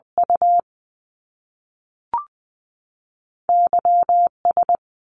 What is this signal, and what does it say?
Morse code, heard as a keyed tone, about 20 wpm, spaced as for 3 wpm, 700 Hz. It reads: UY S
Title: Top Two Letter Combinations in Words 20wpm